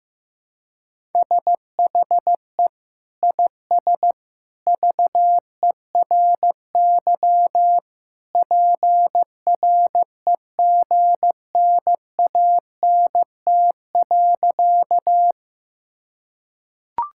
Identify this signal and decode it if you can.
Morse code, heard as a keyed tone, about 15 wpm, 700 Hz. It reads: SHE IS VERY PREGNANT.
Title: Sentences from Top 1000 Words - Review Part 6 - 15wpm